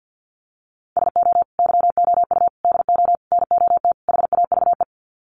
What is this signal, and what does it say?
Morse code, heard as a keyed tone, about 49 wpm, 700 Hz. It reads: SO YOU DO NOT HAVE
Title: Sets of 5 Words Encore 50wpm